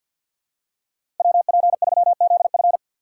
Morse code, heard as a keyed tone, about 53 wpm, 700 Hz. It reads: WP3ZF
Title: USA Call Signs - 5 Characters - 6x Character Spacing - 50wpm